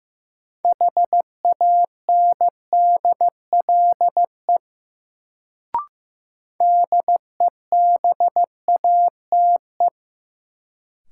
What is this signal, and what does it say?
Morse code, heard as a keyed tone, about 15 wpm, 700 Hz. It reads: HANDLE DEBATE
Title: Sentences from Top 1100 Words - New Word Drill - 15wpm